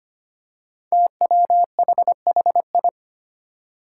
Morse code, heard as a keyed tone, about 25 wpm, 700 Hz. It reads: TWHHI
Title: Five Challenging Letter-Number Combinations 25wpm